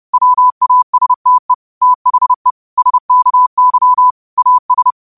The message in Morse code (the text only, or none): WASTE THE SKY AS